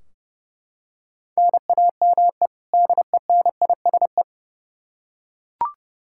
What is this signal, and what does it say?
Morse code, heard as a keyed tone, about 32 wpm, 700 Hz. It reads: NAME DENISE
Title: Practice QSO - Exchange 1 - Beginning - 30wpm